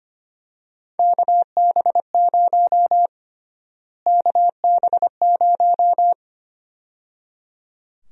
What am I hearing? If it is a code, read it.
Morse code, heard as a keyed tone, about 25 wpm, 700 Hz. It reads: KB0 KB0
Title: USA Call Sign Prefixes - Repeated 2x - 25wpm